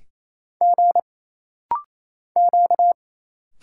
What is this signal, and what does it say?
Morse code, heard as a keyed tone, about 28 wpm, 700 Hz. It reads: G Q
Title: Single Letters - Rapid-Fire 28wpm